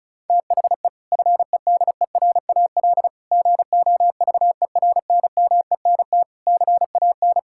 Morse code, heard as a keyed tone, about 36 wpm, 700 Hz. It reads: THE FEDERAL GOVERNMENT CAN
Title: Sentences from Top 1600 Words 35wpm